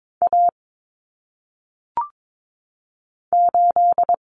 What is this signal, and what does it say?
Morse code, heard as a keyed tone, about 22 wpm, 700 Hz. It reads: A 8
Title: Common Cut Numbers - 1-Digit 22wpm